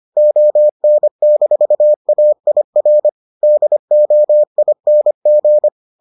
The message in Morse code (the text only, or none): ON-AIR DOING